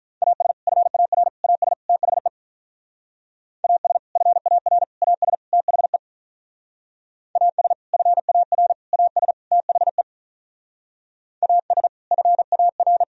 Morse code, heard as a keyed tone, about 45 wpm, 700 Hz. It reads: AS FAR AS THE AS FAR AS THE AS FAR AS THE AS FAR